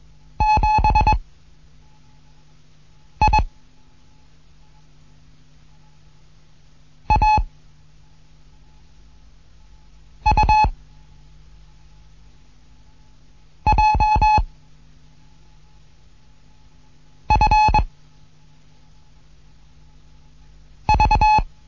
Morse code, heard as a keyed tone, about 21 wpm, 860 Hz. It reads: 7 I A U J F V
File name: Morsen / Kurs A 15